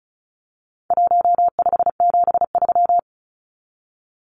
Morse code, heard as a keyed tone, about 34 wpm, 700 Hz. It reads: 1573
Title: Numbers - Contest Style - 4-Digit 35wpm